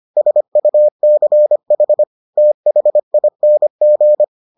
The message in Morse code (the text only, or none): SUCH THING